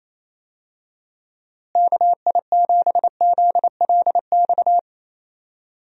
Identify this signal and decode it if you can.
Morse code, heard as a keyed tone, about 28 wpm, 700 Hz. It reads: KI7ZLX
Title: USA Call Signs - 6 Characters - Repeated 2x - 28wpm